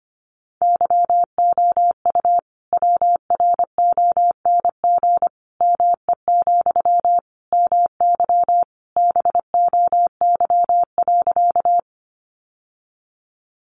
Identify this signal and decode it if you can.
Morse code, heard as a keyed tone, about 25 wpm, 700 Hz. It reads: YOU WRONG ME, MY BOY.
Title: Sentences from Top 900 Words 25wpm